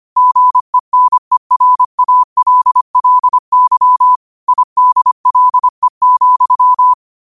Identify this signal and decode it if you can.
Morse code, heard as a keyed tone, about 25 wpm, 1000 Hz. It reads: GENERALLY IDLE,
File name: qotdfiles / 25wpm